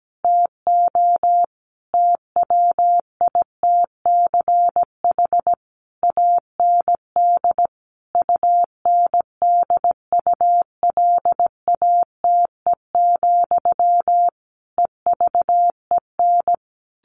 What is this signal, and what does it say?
Morse code, heard as a keyed tone, about 17 wpm, 700 Hz. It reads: TO TWITCH AND UNDULATE, EVEN
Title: A Sound Of Thunder 17wpm